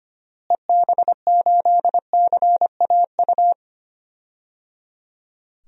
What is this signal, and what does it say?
Morse code, heard as a keyed tone, about 25 wpm, 700 Hz. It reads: EB8CAU
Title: DXCC-100 Call Signs - 6 Characters - Repeated 3x - 25wpm